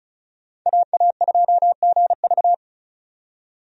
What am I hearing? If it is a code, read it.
Morse code, heard as a keyed tone, about 34 wpm, 700 Hz. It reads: AA2GV
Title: USA Call Signs - 5 Characters - 35wpm